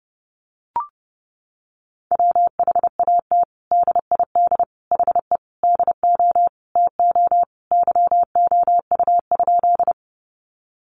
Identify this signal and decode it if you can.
Morse code, heard as a keyed tone, about 30 wpm, 700 Hz. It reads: WHAT DID HE DO TO YOU?
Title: Sentences from Top 100 Words 30wpm